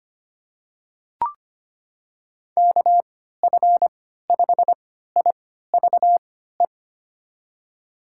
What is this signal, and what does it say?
Morse code, heard as a keyed tone, about 25 wpm, spaced as for 14 wpm, 700 Hz. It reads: KF5IVE
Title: USA Call Signs - 3x Character Spacing - 25wpm